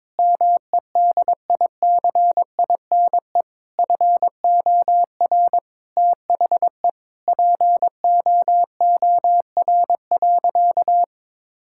Morse code, heard as a keyed tone, about 22 wpm, 700 Hz. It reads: MEDICINE FOR THE POOR.
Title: Sentences from Top 1000 Words - Review Part 6 - 22wpm